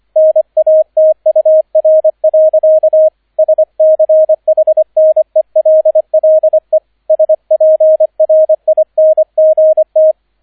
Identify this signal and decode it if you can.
Morse code, heard as a keyed tone, about 25 wpm, 630 Hz, with radio noise. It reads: NATUR. SCHNELLE SPRINGT
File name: 003 Dritter Streich 25wpm